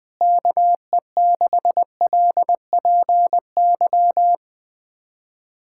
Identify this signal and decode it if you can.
Morse code, heard as a keyed tone, about 21 wpm, 700 Hz. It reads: KE6LPY
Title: USA Callsign DE Callsign - 6 Characters - 20wpm